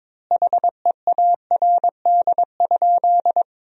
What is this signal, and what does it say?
Morse code, heard as a keyed tone, about 22 wpm, 700 Hz. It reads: HEARD?
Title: 2x Word-Spacing - Sentences from Top 400 Words - 22wpm